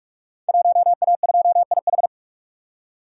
Morse code, heard as a keyed tone, about 44 wpm, 690 Hz. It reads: 1A2IH
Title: Code Groups - 5 Letter:Numbers - 45wpm